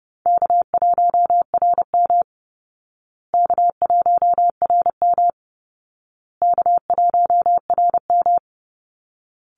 Morse code, heard as a keyed tone, about 30 wpm, 700 Hz. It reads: K1RM K1RM K1RM
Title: Easy Call Signs Repeated 3x - 30wpm